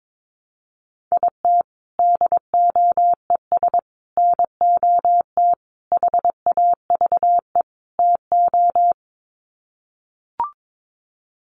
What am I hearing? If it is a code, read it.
Morse code, heard as a keyed tone, about 22 wpm, 700 Hz. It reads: IT DOES NOT HAVE TO
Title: Sets of 5 Words 22wpm